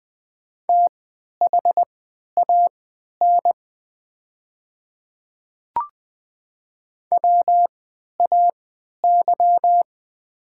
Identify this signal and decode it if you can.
Morse code, heard as a keyed tone, about 20 wpm, spaced as for 11 wpm, 700 Hz. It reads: THAN WAY
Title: 3x Character Spacing - Top 100 Words - 20wpm